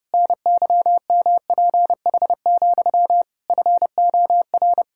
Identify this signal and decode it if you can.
Morse code, heard as a keyed tone, about 30 wpm, 700 Hz. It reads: NYMPH, FOR
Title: Pangram Challenge v2 - 30wpm